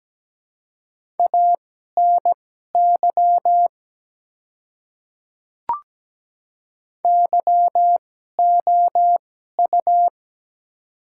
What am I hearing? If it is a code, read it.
Morse code, heard as a keyed tone, about 17 wpm, spaced as for 12 wpm, 700 Hz. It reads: ANY YOU
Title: 2x Character Spacing - Top 200 Words - 17wpm